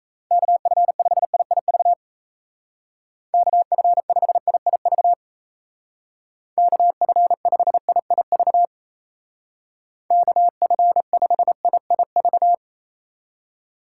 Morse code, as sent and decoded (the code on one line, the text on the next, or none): -.- ..-. ..... .. .. ...- / -.- ..-. ..... .. .. ...- / -.- ..-. ..... .. .. ...- / -.- ..-. ..... .. .. ...-
KF5IIV KF5IIV KF5IIV KF5IIV